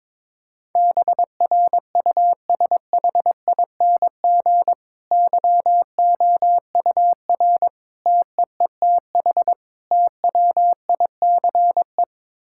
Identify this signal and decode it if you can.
Morse code, heard as a keyed tone, about 22 wpm, 700 Hz. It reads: BRUSHING YOUR TEETH TWICE
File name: Difficult Sentences with Hints - 22wpm